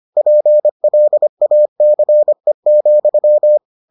Morse code, heard as a keyed tone, about 25 wpm, 600 Hz. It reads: PLACE,